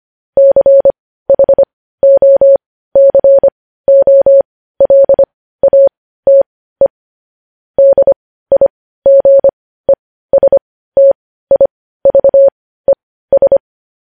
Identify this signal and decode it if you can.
Morse code, heard as a keyed tone, about 25 wpm, spaced as for 15 wpm, 550 Hz. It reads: CHOCOLATE DIGESTIVES
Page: news headlines in morse code at 15 wpm